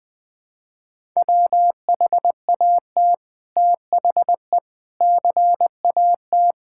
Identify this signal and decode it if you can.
Morse code, heard as a keyed tone, about 19 wpm, 700 Hz. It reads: WHAT THE CAT
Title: Sentences with 2-Word Phrases - Spoken Beginning - 20wpm